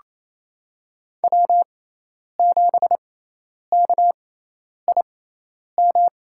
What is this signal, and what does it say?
Morse code, heard as a keyed tone, about 28 wpm, spaced as for 10 wpm, 700 Hz. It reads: W7KIM
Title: USA Call Signs - 5 Characters - 6x Character Spacing - 28wpm